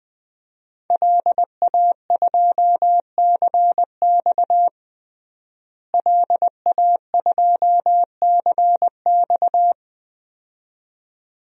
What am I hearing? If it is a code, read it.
Morse code, heard as a keyed tone, about 20 wpm, 700 Hz. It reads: LA2CX LA2CX
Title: DXCC-100 Call Signs - 5 Characters - Repeated 2x - 20wpm